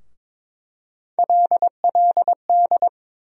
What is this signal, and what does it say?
Morse code, heard as a keyed tone, about 22 wpm, 700 Hz. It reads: LLD
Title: USA Call Sign Postfixes - 3 Letters 22wpm